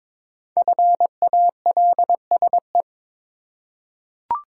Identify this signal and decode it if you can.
Morse code, heard as a keyed tone, about 22 wpm, 700 Hz. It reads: FALSE
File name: 5 Letter Words - First 3 Letters Given - 22wpm